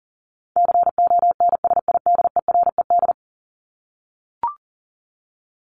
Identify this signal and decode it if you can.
Morse code, heard as a keyed tone, about 40 wpm, 700 Hz. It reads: CONSIDERED
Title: ED Suffix - Top 100 Words with Base Words 40wpm